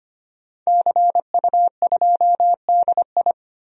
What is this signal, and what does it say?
Morse code, heard as a keyed tone, about 25 wpm, 700 Hz. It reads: CU2DI
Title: DXCC-100 Call Signs - 5 Characters - Repeated 2x - 25wpm